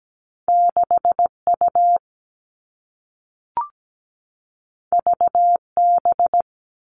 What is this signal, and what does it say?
Morse code, heard as a keyed tone, about 17 wpm, 700 Hz. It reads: 6U VB
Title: All Cut Numbers - 2-Digit 17wpm